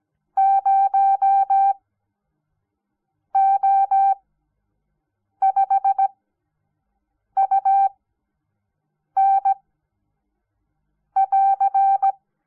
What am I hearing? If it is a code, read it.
Morse code, heard as a keyed tone, about 17 wpm, spaced as for 5 wpm, 780 Hz. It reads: 0O5UN<AR>
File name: Onlinelehrgang / morsen / DJ4UF Morsen06